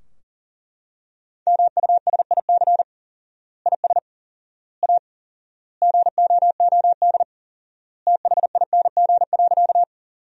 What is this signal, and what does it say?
Morse code, heard as a keyed tone, about 37 wpm, 700 Hz. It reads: MUSIC IS A GOOD THING.
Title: 4x Word-Spacing - Sentences from Top 300 Words - 40wpm